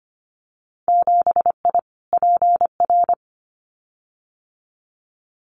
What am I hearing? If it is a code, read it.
Morse code, heard as a keyed tone, about 25 wpm, 700 Hz. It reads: 7I PR
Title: Winter Field Day Exchanges Encore 25wpm